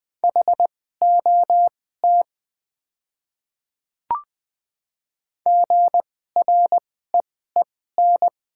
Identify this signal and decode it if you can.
Morse code, heard as a keyed tone, about 20 wpm, spaced as for 14 wpm, 700 Hz. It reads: HOT GREEN
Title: 2x Character Spacing - Top 500 Words - 20wpm